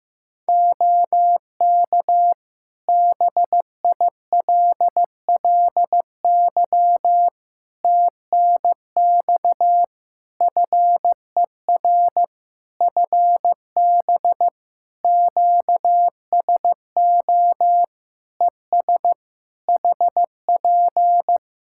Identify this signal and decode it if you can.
Morse code, heard as a keyed tone, about 15 wpm, 700 Hz. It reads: OK BILLY TNX FER FB QSO ES HP
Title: Practice QSO - Exchange 4 - Ending - 15wpm